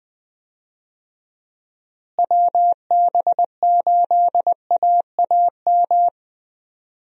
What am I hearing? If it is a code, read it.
Morse code, heard as a keyed tone, about 20 wpm, 700 Hz. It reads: WB8AAM
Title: USA Call Signs - 6 Characters - Repeated 2x - 20wpm